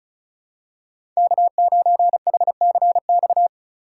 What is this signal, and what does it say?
Morse code, heard as a keyed tone, about 34 wpm, 700 Hz. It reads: K9HCX
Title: USA Call Signs - 5 Characters - 2x Character Spacing - 35wpm